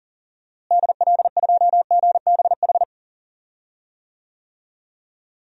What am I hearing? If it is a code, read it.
Morse code, heard as a keyed tone, about 39 wpm, 700 Hz. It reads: DL2GBH
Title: DXCC-100 Call Signs - 6 Characters - Repeated 3x - 40wpm